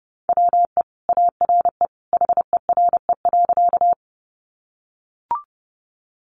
Morse code, heard as a keyed tone, about 30 wpm, 700 Hz. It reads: WE ARE HERE.
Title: Sentences from Top 200 Words 30wpm